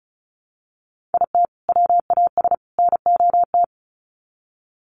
Morse code, heard as a keyed tone, about 34 wpm, 700 Hz. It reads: IT WAS NOT
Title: Sets of 3 Words 35wpm